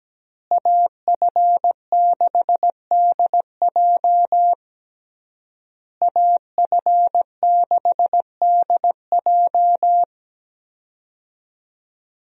Morse code, heard as a keyed tone, about 17 wpm, 700 Hz. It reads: AF6DJ AF6DJ
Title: USA Call Signs - 5 Characters - Repeated 3x - 17wpm